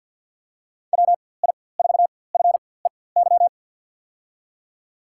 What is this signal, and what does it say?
Morse code, heard as a keyed tone, about 52 wpm, spaced as for 24 wpm, 700 Hz. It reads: WI4FEY
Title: USA Call Signs - 4x Character Spacing - 50wpm